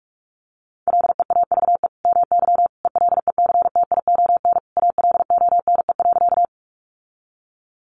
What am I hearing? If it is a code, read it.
Morse code, heard as a keyed tone, about 44 wpm, 700 Hz. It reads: LEAVE MY ELECTION ALONE.
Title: Sentences from Top 1500 Words 45wpm